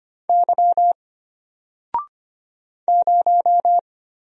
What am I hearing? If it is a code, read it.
Morse code, heard as a keyed tone, about 25 wpm, 700 Hz. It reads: Y 0
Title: Single Letter-Number Alt 25wpm